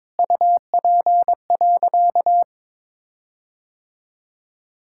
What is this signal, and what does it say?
Morse code, heard as a keyed tone, about 22 wpm, 700 Hz. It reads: UP.
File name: Sentences with 7-Word Phrases - 22wpm